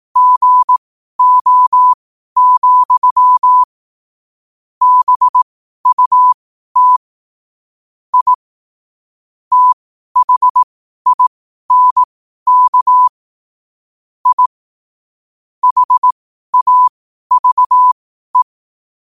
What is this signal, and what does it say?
Morse code, heard as a keyed tone, about 18 wpm, spaced as for 12 wpm, 1000 Hz. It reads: GO, BUT I THINK I HAVE
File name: qotdfiles / 12wpm